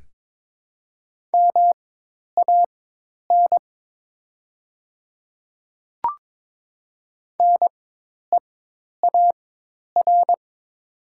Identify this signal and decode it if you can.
Morse code, heard as a keyed tone, about 22 wpm, spaced as for 10 wpm, 700 Hz. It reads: MAN NEAR
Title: 4x Character Spacing - Top 500 Words - 22wpm